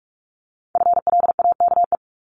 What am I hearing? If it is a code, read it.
Morse code, heard as a keyed tone, about 46 wpm, 700 Hz. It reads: FLAKE
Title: Multi-letter Phonemes Repeated 3x 45wpm